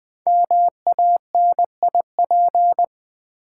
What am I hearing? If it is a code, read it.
Morse code, heard as a keyed tone, about 20 wpm, 700 Hz. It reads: MANIP
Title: Build A Word - 5 Letters At A Time - 12-Letter Words - 20wpm